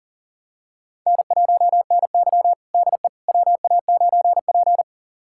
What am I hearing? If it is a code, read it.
Morse code, heard as a keyed tone, about 37 wpm, 700 Hz. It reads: N1NY DE WA9P
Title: USA Callsign DE Callsign - 4 Characters - 40wpm